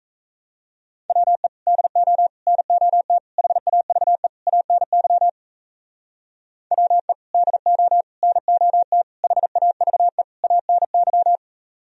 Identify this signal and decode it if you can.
Morse code, heard as a keyed tone, about 36 wpm, 700 Hz. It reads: WE DO NOT HAVE ANY WE DO NOT HAVE ANY